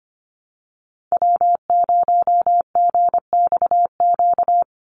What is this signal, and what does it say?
Morse code, heard as a keyed tone, about 25 wpm, 700 Hz. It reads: W0GXQ
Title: Call Signs - 25wpm